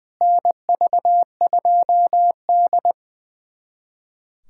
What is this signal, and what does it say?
Morse code, heard as a keyed tone, about 20 wpm, 700 Hz. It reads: NV2D
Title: USA Callsign DE Callsign - 4 Characters - 20wpm